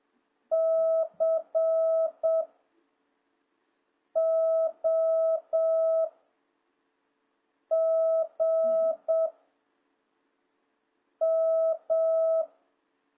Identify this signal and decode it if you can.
Morse code, heard as a keyed tone, about 7 wpm, 650 Hz. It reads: C O G M